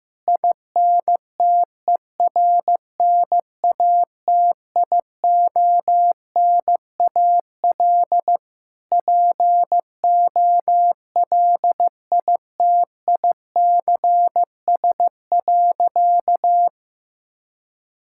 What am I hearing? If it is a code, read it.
Morse code, heard as a keyed tone, about 15 wpm, 700 Hz. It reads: INTERNATIONAL POLITICS.
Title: Sentences from Top 1000 Words - Review Part 6 - 15wpm